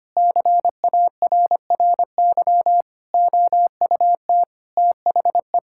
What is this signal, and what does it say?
Morse code, heard as a keyed tone, about 25 wpm, 700 Hz. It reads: CARRY OUT THE
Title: Sentences from Top 1600 Words 25wpm